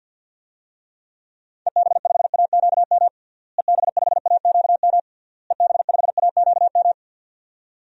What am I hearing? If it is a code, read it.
Morse code, heard as a keyed tone, about 47 wpm, 700 Hz. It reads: EB5AQM EB5AQM EB5AQM
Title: DXCC-100 Call Signs - 6 Characters - Repeated 3x - 50wpm